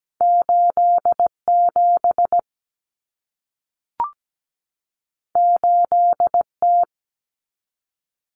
Morse code, heard as a keyed tone, about 17 wpm, 700 Hz. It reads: 87 8T
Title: Common Cut Numbers - 2-Digit 17wpm